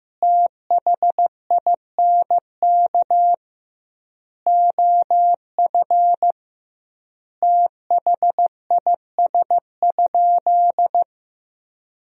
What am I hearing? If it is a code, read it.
Morse code, heard as a keyed tone, about 15 wpm, 700 Hz. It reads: THINK OF THIS?
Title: 2x Word-Spacing - Sentences from Top 200 Words - 15wpm